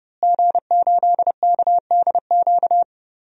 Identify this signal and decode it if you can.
Morse code, heard as a keyed tone, about 32 wpm, 700 Hz. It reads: G8KDQ
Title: DXCC-100 Call Signs - 5 Characters - Repeated 2x - 30wpm